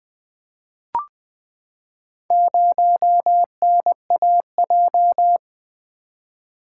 Morse code, heard as a keyed tone, about 20 wpm, 700 Hz. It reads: 0NAJ
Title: Four Letter-Number Combinations - 20wpm